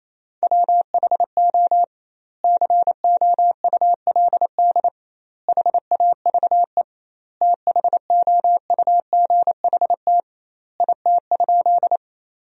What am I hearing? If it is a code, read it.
Morse code, heard as a keyed tone, about 27 wpm, 700 Hz. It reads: WHO COULD HAVE THOUGHT IT?
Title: 2x Word-Spacing - Sentences from Top 300 Words - 28wpm